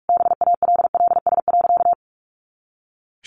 Morse code, heard as a keyed tone, about 44 wpm, 700 Hz. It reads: BALLS.
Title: Easy Phrases 45wpm